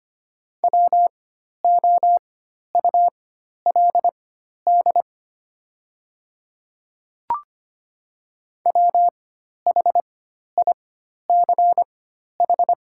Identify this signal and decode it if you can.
Morse code, heard as a keyed tone, about 25 wpm, spaced as for 12 wpm, 700 Hz. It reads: WOULD WHICH
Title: 4x Character Spacing - Top 200 Words - 25wpm